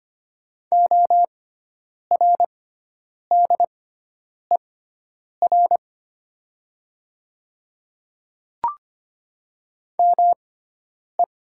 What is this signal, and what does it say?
Morse code, heard as a keyed tone, about 25 wpm, spaced as for 9 wpm, 700 Hz. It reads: ORDER ME
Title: 6x Character Spacing - Top 500 Words - 25wpm